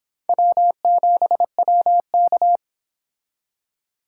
Morse code, heard as a keyed tone, about 26 wpm, 700 Hz. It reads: W7WK